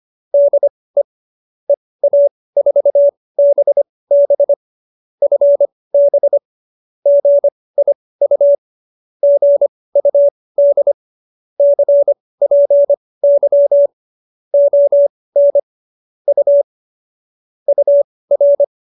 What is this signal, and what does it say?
Morse code, heard as a keyed tone, about 25 wpm, spaced as for 18 wpm, 570 Hz. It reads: DE EA4BB FB GIU GUD CPY ON U UR
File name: qso215b 18